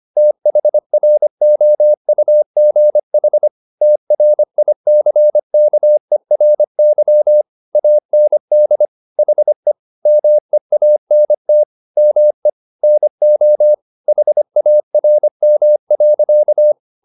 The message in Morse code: THROUGH TRICKERY AND HE MEANT ME NO HARM.

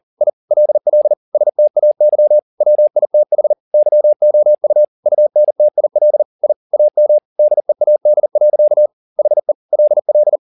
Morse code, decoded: I LL STAY WITH YOU UNTIL I AM DEAD. HE LL

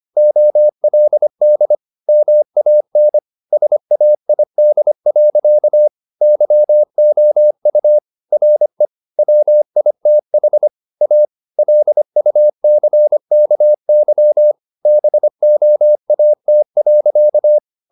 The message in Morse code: OLD MAN SAID. YOU RE WITH A LUCKY BOAT.

